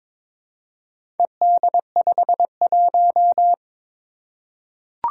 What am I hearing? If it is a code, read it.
Morse code, heard as a keyed tone, about 22 wpm, 700 Hz. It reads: ED51
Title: Four Challenging Letter-Number Combinations 22wpm